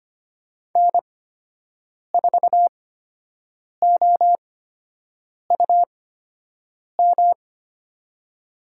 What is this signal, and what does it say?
Morse code, heard as a keyed tone, about 25 wpm, spaced as for 7 wpm, 700 Hz. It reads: N4OUM
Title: USA Call Signs - 8x Character Spacing - 25wpm